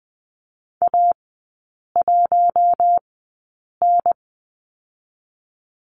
Morse code, heard as a keyed tone, about 20 wpm, spaced as for 8 wpm, 700 Hz. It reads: A1N
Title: 060 Lesson 11e - Letter h - Three Characters 20wpm